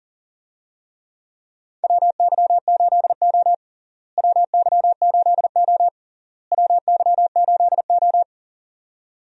Morse code, heard as a keyed tone, about 38 wpm, 700 Hz. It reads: WY8O WY8O WY8O
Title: USA Call Signs - 4 Characters - Repeated 3x - 40wpm